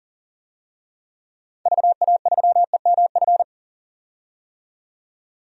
Morse code, heard as a keyed tone, about 40 wpm, 700 Hz. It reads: VA3EMF
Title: DXCC-100 Call Signs - 6 Characters - 40wpm